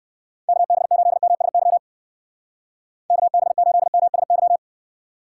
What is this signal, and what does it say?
Morse code, heard as a keyed tone, about 59 wpm, 700 Hz. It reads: KB8MSY KB8MSY